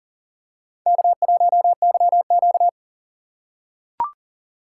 Morse code, heard as a keyed tone, about 43 wpm, 700 Hz. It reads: K1YQ
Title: USA Call Signs - 4 Characters - 6x Character Spacing - 40wpm